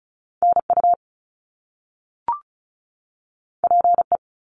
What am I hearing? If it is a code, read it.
Morse code, heard as a keyed tone, about 36 wpm, 700 Hz. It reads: NU PE
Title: Canadian Provinces and Territories 35wpm